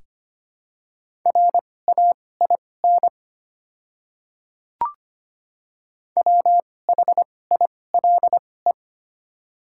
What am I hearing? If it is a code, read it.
Morse code, heard as a keyed tone, about 25 wpm, spaced as for 18 wpm, 700 Hz. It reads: RAIN WHILE
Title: 2x Character Spacing - Top 300 Words - 25wpm